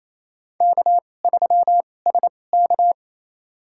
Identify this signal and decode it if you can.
Morse code, heard as a keyed tone, about 28 wpm, spaced as for 20 wpm, 700 Hz. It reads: K3SK
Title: USA Call Signs - 2x Character Spacing - 28wpm